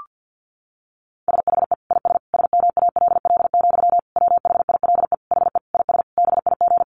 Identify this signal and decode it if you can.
Morse code, heard as a keyed tone, about 48 wpm, 700 Hz. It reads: SHE IS SMALL, WHILE HE IS BIG
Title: Sentences from Top 1300 Words 50wpm